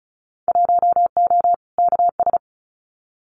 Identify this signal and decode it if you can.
Morse code, heard as a keyed tone, about 36 wpm, 690 Hz. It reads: 1O KS
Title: Winter Field Day Exchanges 35wpm